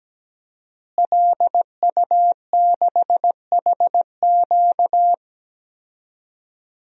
Morse code, heard as a keyed tone, about 17 wpm, 700 Hz. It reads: LU6HQ
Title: DXCC-100 Call Signs - 5 Characters - Repeated 2x - 17wpm